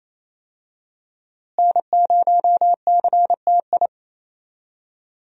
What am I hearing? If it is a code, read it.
Morse code, heard as a keyed tone, about 28 wpm, 700 Hz. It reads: N0CTI
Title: USA Call Signs - 5 Characters - Repeated 3x - 28wpm